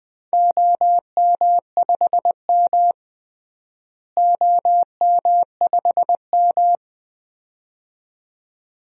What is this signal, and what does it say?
Morse code, heard as a keyed tone, about 21 wpm, 700 Hz. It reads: OM5M OM5M
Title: DXCC-100 Call Signs - 4 Characters - Repeated 2x - 20wpm